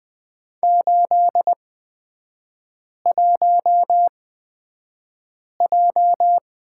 Morse code, heard as a keyed tone, about 20 wpm, spaced as for 5 wpm, 690 Hz. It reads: 81J
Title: 236 Lesson 38e - Letter q - Three Characters 20wpm